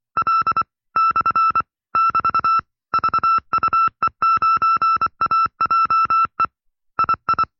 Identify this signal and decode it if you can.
Morse code, heard as a keyed tone, about 24 wpm, 1330 Hz. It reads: L / <BT> VUE9AJE II
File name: FreeFiles / CW-practice / 20WPM-side2